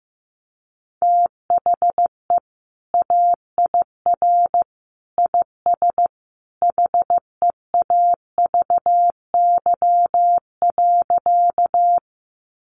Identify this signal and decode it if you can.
Morse code, heard as a keyed tone, about 15 wpm, 700 Hz. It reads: THE AIR IS HEAVY.
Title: Sentences from Top 1100 Words 15wpm